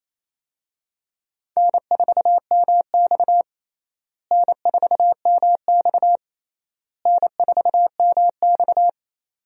Morse code, heard as a keyed tone, about 28 wpm, 700 Hz. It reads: N4MX N4MX N4MX
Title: USA Call Signs - 4 Characters - Repeated 3x - 28wpm